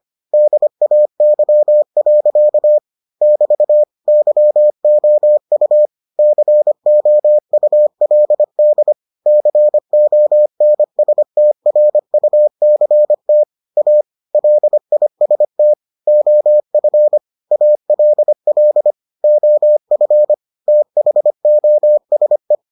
Morse code, heard as a keyed tone, about 25 wpm, 600 Hz. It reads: DAY. <BT> YOU COULD CONSTRUCT A LIST OF ALL OF THOSE